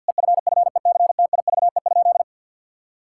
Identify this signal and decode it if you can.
Morse code, heard as a keyed tone, about 48 wpm, 700 Hz. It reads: EFFECTIVE?
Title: Sentences from Top 1000 Words - Review Part 6 - 50wpm